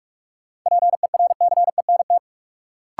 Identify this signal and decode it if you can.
Morse code, heard as a keyed tone, about 45 wpm, 700 Hz. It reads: PERCENT
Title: Sets of 2 Words - New Word Drill - 45wpm